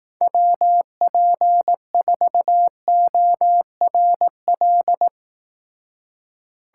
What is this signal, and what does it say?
Morse code, heard as a keyed tone, about 19 wpm, 700 Hz. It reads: WP4ORL
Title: Speed Racing - USA 2x3 Call Signs 15wpm